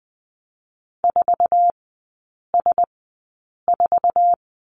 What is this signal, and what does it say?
Morse code, heard as a keyed tone, about 20 wpm, spaced as for 8 wpm, 700 Hz. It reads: 4S4
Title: 047 Lesson 9e - Number 4 - Three Characters 20wpm